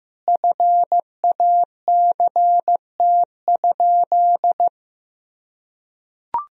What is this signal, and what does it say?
Morse code, heard as a keyed tone, about 16 wpm, 700 Hz. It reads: FACT?
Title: Sentences from Top 1000 Words - Review Part 4 - 15wpm